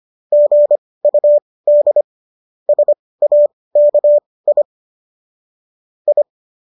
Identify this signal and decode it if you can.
Morse code, heard as a keyed tone, about 25 wpm, spaced as for 18 wpm, 590 Hz. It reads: GUD SAKI I
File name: qso218 18